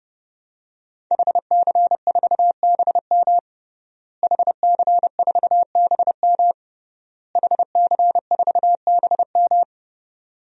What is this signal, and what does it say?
Morse code, heard as a keyed tone, about 30 wpm, 700 Hz. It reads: HC4BM HC4BM HC4BM
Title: DXCC-100 Call Signs - 5 Characters - Repeated 3x - 30wpm